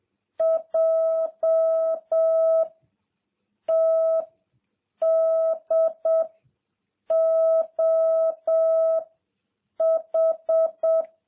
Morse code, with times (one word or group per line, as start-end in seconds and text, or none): 0.4-2.6 J
3.7-4.2 T
5.0-6.2 D
7.1-9.0 O
9.8-11.0 H